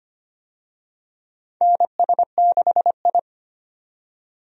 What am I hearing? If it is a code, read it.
Morse code, heard as a keyed tone, about 25 wpm, 700 Hz. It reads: NS6I
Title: USA Call Signs - 4 Characters - 25wpm